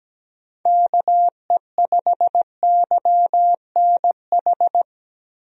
Morse code, heard as a keyed tone, about 17 wpm, 700 Hz. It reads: KE5YNH
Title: USA Call Signs - 6 Characters - Repeated 3x - 17wpm